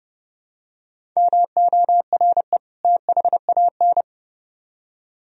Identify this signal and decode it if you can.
Morse code, heard as a keyed tone, about 29 wpm, 700 Hz. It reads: MORE THAN